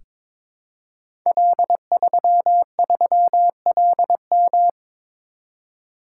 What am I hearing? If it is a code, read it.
Morse code, heard as a keyed tone, about 22 wpm, 700 Hz. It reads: L33LM
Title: Code Groups - 5 Letter:Numbers - 22wpm